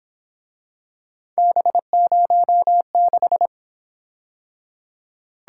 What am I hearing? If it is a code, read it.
Morse code, heard as a keyed tone, about 28 wpm, 700 Hz. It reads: B06